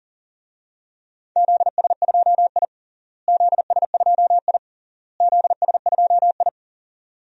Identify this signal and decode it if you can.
Morse code, heard as a keyed tone, about 40 wpm, 700 Hz. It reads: ZS2I ZS2I ZS2I
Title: DXCC-100 Call Signs - 4 Characters - Repeated 3x - 40wpm